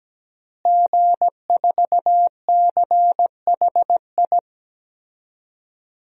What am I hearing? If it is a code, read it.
Morse code, heard as a keyed tone, about 17 wpm, 700 Hz. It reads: G4CHI
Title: DXCC-100 Call Signs - 5 Characters - Repeated 2x - 17wpm